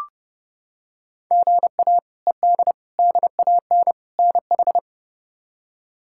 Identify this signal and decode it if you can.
Morse code, heard as a keyed tone, about 32 wpm, 700 Hz. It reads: GA ED DAN NH
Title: K1USN SST QSO Practice - 30wpm